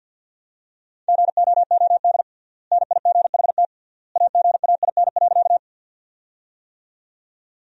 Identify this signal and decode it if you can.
Morse code, heard as a keyed tone, about 50 wpm, 700 Hz. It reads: GOOD NIGHT AGAIN.
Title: 3x Word-Spacing - Sentences from Top 300 Words - 50wpm